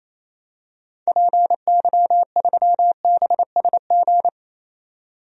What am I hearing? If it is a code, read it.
Morse code, heard as a keyed tone, about 28 wpm, 700 Hz. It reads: PY3BSG
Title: DXCC-100 Call Signs - 6 Characters - 28wpm